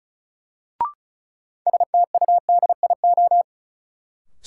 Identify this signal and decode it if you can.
Morse code, heard as a keyed tone, about 36 wpm, 700 Hz. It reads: STUDIO
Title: Sentences from Top 1200 Words - New Word Drill Alt - 35wpm